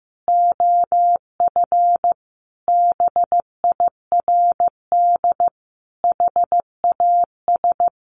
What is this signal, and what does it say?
Morse code, heard as a keyed tone, about 15 wpm, 700 Hz. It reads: OF BIRD HAS
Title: Sentences from Top 1200 Words 15wpm